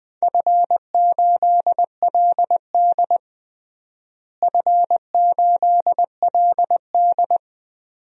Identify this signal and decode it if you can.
Morse code, heard as a keyed tone, about 20 wpm, 700 Hz. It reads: F8LD F8LD
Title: DXCC-100 Call Signs - 4 Characters - Repeated 2x - 20wpm